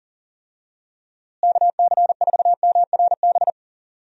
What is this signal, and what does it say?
Morse code, heard as a keyed tone, about 40 wpm, 700 Hz. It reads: KC4MRB
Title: USA Call Signs - 6 Characters - Repeated 2x - 40wpm